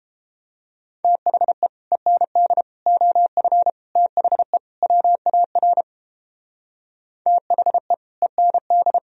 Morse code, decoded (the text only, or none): THE END OF THE WAR THE END